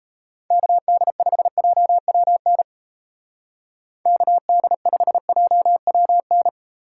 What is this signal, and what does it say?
Morse code, heard as a keyed tone, about 35 wpm, 700 Hz. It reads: KD5JWN KD5JWN
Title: Speed Racing - USA 2x3 Call Signs 28wpm